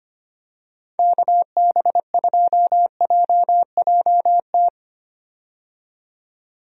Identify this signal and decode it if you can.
Morse code, heard as a keyed tone, about 25 wpm, 700 Hz. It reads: KB2JJT
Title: USA Call Signs - 6 Characters - 25wpm